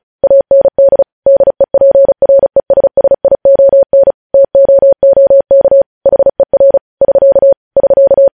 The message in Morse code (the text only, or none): AND DEPRESSION TOOK HER <SK> <SK>